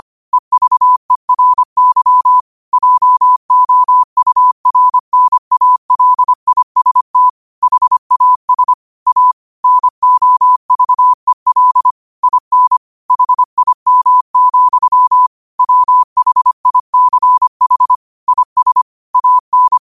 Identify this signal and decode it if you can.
Morse code, heard as a keyed tone, about 25 wpm, 1000 Hz. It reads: EVERY JOURNALIST HAS A NOVEL IN HIM, WHICH IS AN